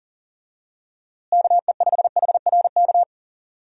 Morse code, heard as a keyed tone, about 40 wpm, 700 Hz. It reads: KE5HRK
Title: USA Call Signs - 6 Characters - Repeated 2x - 40wpm